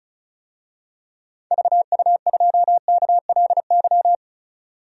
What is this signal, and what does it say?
Morse code, heard as a keyed tone, about 34 wpm, 700 Hz. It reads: VU2KLY
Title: DXCC-100 Call Signs - 6 Characters - 35wpm